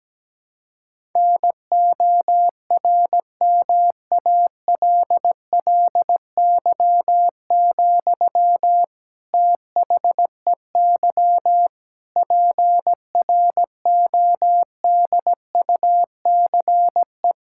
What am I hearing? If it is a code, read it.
Morse code, heard as a keyed tone, about 17 wpm, 700 Hz. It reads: NORMALLY, THEY PRODUCE
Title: Sentences from Top 1000 Words - Review Part 6 - 17wpm